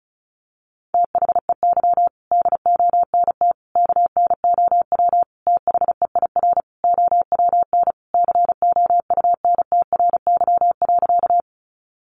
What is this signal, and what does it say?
Morse code, heard as a keyed tone, about 34 wpm, 700 Hz. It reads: THEY DONT KNOW THEIR OWN COUNTRY.
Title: Sentences from Top 300 Words 35wpm